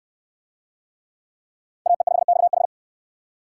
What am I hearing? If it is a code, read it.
Morse code, heard as a keyed tone, about 62 wpm, 700 Hz. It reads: AE5XH